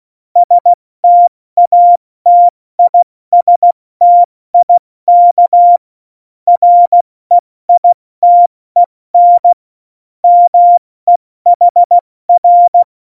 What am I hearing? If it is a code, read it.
Morse code, heard as a keyed tone, about 16 wpm, 700 Hz. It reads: STATISTIK REITEN MEHR
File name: Morsekurs / Witze / Witz 26. Januar 2021, 70 F80